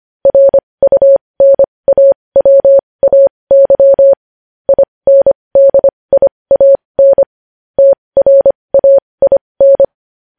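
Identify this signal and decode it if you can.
Morse code, heard as a keyed tone, about 25 wpm, spaced as for 20 wpm, 550 Hz. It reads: RUNAWAY INDIAN TRAIN
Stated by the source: Morse code news headlines sent at 20.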